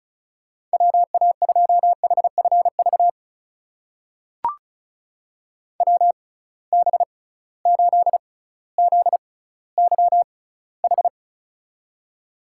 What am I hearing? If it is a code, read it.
Morse code, heard as a keyed tone, about 34 wpm, spaced as for 18 wpm, 700 Hz. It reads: WA2HFV WB8ZYH
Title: USA Call Signs - 6 Characters - 6x Character Spacing - 35wpm